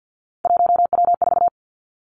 Morse code, heard as a keyed tone, about 50 wpm, 700 Hz. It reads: 1W4
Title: Three Challenging Letter-Number Combinations 50wpm